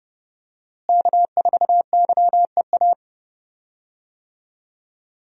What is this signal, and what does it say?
Morse code, heard as a keyed tone, about 30 wpm, 700 Hz. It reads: K4YEA
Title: USA Call Signs - 5 Characters - Repeated 3x - 30wpm